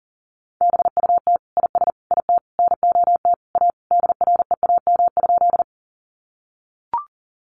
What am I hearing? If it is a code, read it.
Morse code, heard as a keyed tone, about 40 wpm, 700 Hz. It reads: BUT IS IT NOT A DREAM?
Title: Sentences from Top 1500 Words 40wpm